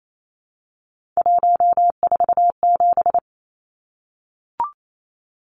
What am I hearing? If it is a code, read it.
Morse code, heard as a keyed tone, about 28 wpm, 700 Hz. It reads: 147
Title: Numbers - 3-Digit 28wpm